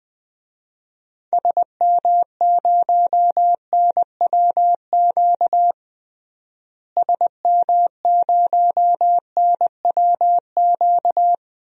Morse code, heard as a keyed tone, about 20 wpm, 700 Hz. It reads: SM0NWQ SM0NWQ
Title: DXCC-100 Call Signs - 6 Characters - Repeated 3x - 20wpm